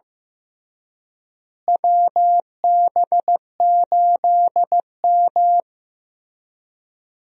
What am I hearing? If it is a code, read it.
Morse code, heard as a keyed tone, about 16 wpm, 700 Hz. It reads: WB8M
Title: USA Call Signs - 4 Characters - Repeated 3x - 15wpm